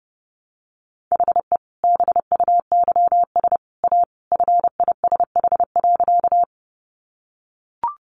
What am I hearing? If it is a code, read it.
Morse code, heard as a keyed tone, about 30 wpm, 700 Hz. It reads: HE BUYS A FISH.
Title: Easy Phrases 30wpm